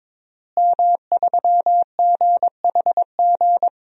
Morse code, heard as a keyed tone, about 22 wpm, 700 Hz. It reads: M3GHG
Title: DXCC-100 Call Signs - 5 Characters - Repeated 2x - 22wpm